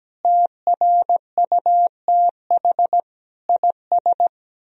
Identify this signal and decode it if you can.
Morse code, heard as a keyed tone, about 17 wpm, 700 Hz. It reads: TRUTH IS
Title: Sentences from Top 1000 Words - Review Part 3 - 17wpm